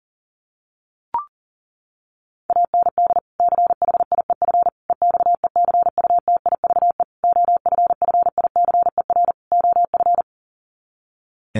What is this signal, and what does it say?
Morse code, heard as a keyed tone, about 40 wpm, 700 Hz. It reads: AND CHIEF EXECUTIVE OFFICER OF
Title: Sets of 5 Words Encore2 40wpm